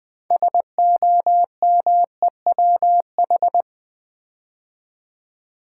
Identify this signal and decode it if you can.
Morse code, heard as a keyed tone, about 20 wpm, 700 Hz. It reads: SOMEWH
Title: Build A Word - 3 Letters At A Time - 9-Letter Words - 20wpm